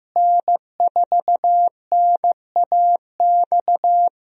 Morse code, heard as a keyed tone, about 16 wpm, 700 Hz. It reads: N4NAX
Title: USA Call Signs - 5 Characters - Repeated 3x - 15wpm